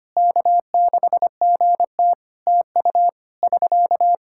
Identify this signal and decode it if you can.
Morse code, heard as a keyed tone, about 25 wpm, 700 Hz. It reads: K6GT TU <SK>
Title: Practice QSO - Exchange 4 - Ending - 25wpm